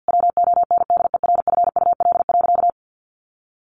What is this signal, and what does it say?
Morse code, heard as a keyed tone, about 48 wpm, 700 Hz. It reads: WONDERFUL.
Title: Sentences from Top 1400 Words 50wpm